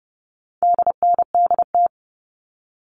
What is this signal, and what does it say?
Morse code, heard as a keyed tone, about 30 wpm, 700 Hz. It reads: DNDT
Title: All Cut Numbers - 4-Digit 30wpm